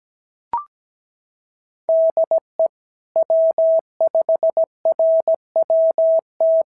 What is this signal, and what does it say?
Morse code, heard as a keyed tone, about 17 wpm, 660 Hz. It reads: DE W5RWT
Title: SOTA QSO Practice 17wpm